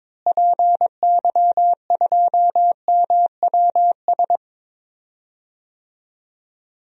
Morse code, heard as a keyed tone, about 22 wpm, 700 Hz. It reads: PY2MWS
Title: DXCC-100 Call Signs - 6 Characters - Repeated 2x - 22wpm